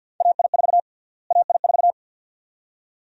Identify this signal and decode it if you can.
Morse code, heard as a keyed tone, about 51 wpm, 700 Hz. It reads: AI4 AI4
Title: USA Call Sign Prefixes - Repeated 2x - 50wpm